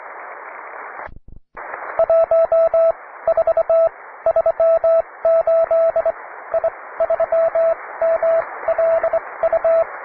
Morse code, heard as a keyed tone, about 22 wpm, spaced as for 17 wpm, 660 Hz, with radio noise. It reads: 1438 I3MLU